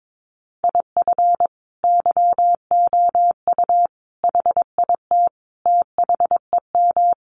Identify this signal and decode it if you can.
Morse code, heard as a keyed tone, about 22 wpm, 700 Hz. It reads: IF YOU HIT THEM
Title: A Sound Of Thunder 22wpm